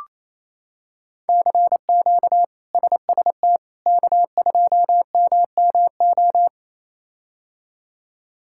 Morse code, heard as a keyed tone, about 27 wpm, 700 Hz. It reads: CQ SST K2MMO
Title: K1USN SST QSO Practice - 28wpm